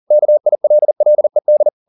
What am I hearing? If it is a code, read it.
Morse code, heard as a keyed tone, about 40 wpm, 600 Hz. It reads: KILLED
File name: the old man and the sea 2 40wpm